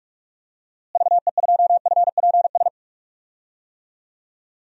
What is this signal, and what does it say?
Morse code, heard as a keyed tone, about 46 wpm, 700 Hz. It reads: VE2FPS
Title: DXCC-100 Call Signs - 6 Characters - Repeated 3x - 45wpm